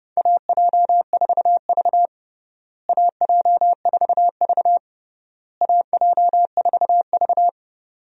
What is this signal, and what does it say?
Morse code, heard as a keyed tone, about 30 wpm, 700 Hz. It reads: AJ4V AJ4V AJ4V
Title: USA Call Signs - 4 Characters - Repeated 3x - 30wpm